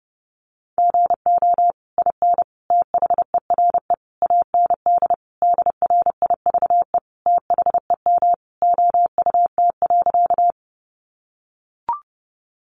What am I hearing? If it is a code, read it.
Morse code, heard as a keyed tone, about 30 wpm, 700 Hz. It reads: GO IN THERE AND DRIVE THEM OUT.
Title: Sentences from Top 500 Words 30wpm